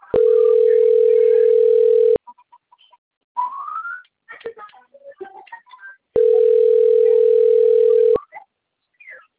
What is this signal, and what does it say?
Ringback tone (North America), 440 + 480 Hz.
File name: out- - - - - .91098